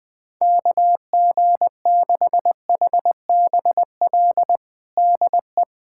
Morse code, heard as a keyed tone, about 20 wpm, 700 Hz. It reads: KG6HBL DE
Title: K1USN SST QSO Practice 20wpm